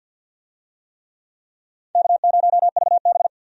Morse code, heard as a keyed tone, about 50 wpm, 700 Hz. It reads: K0VB
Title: USA Call Signs - 4 Characters - 50wpm